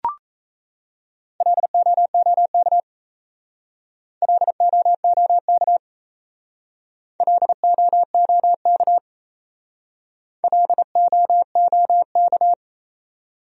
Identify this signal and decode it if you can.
Morse code, heard as a keyed tone, about 35 wpm, 700 Hz. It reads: LOOK LOOK LOOK LOOK
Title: Speed-Racing Top 100 Words 28wpm